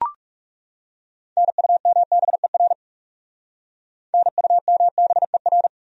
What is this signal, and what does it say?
Morse code, heard as a keyed tone, about 38 wpm, 700 Hz. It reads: NUMBER NUMBER